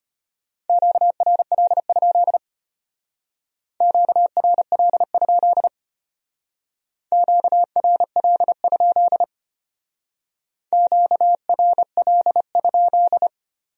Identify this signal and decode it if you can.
Morse code, heard as a keyed tone, about 32 wpm, 700 Hz. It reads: QRL? QRL? QRL? QRL?